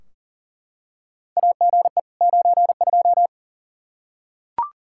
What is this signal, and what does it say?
Morse code, heard as a keyed tone, about 39 wpm, 700 Hz. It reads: AGE 92
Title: Practice QSO - Exchange 3 - More Info - 40wpm